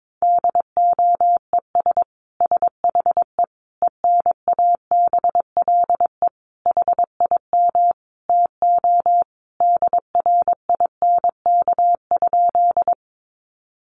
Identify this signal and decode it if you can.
Morse code, heard as a keyed tone, about 22 wpm, 700 Hz. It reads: DOES SHE ENABLE HIM TO DRINK?
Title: Sentences from Top 1500 Words 22wpm